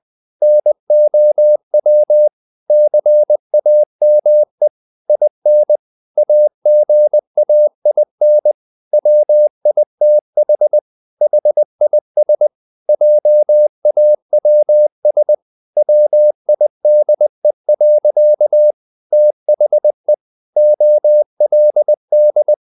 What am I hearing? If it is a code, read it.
Morse code, heard as a keyed tone, about 20 wpm, 600 Hz. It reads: NOW CAME IN AGAIN WITH HIS JAWS WIDE. THE OLD
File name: the old man and the sea part 2 20wpm